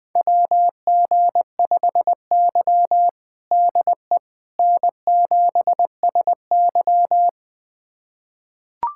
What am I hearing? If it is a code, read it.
Morse code, heard as a keyed tone, about 20 wpm, 700 Hz. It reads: WG5Y DE N7SY
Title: USA Callsign DE Callsign - 4 Characters - 20wpm